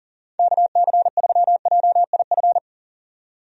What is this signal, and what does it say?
Morse code, heard as a keyed tone, about 39 wpm, 700 Hz. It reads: KC3JIF
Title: USA Call Signs - 6 Characters - 40wpm